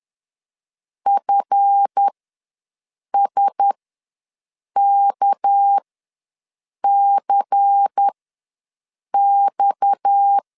Morse code, heard as a keyed tone, about 10 wpm, spaced as for 6 wpm, 800 Hz. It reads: FSKCX